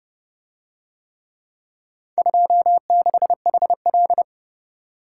Morse code, heard as a keyed tone, about 29 wpm, 700 Hz. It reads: 26HL